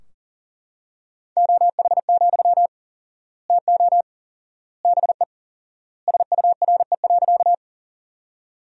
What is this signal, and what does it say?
Morse code, heard as a keyed tone, about 40 wpm, 700 Hz. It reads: OH, TO BE SURE.
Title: 4x Word-Spacing - Sentences from Top 500 Words - 40wpm